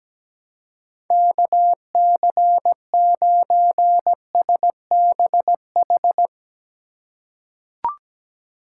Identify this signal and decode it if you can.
Morse code, heard as a keyed tone, about 17 wpm, 700 Hz. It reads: KC9SBH
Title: USA Call Signs - 2x Character Spacing - 17wpm